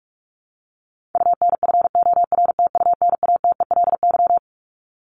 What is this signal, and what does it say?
Morse code, heard as a keyed tone, about 44 wpm, 700 Hz. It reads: UNFORTUNATELY
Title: Top 1751-2000 Words 45wpm